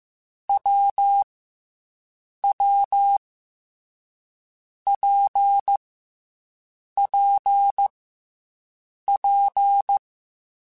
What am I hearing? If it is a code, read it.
Morse code, heard as a keyed tone, about 15 wpm, 800 Hz. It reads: W W P P P
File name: Difficult Combos / Difficult Combos-mp3s / WPJ WPJ1 @13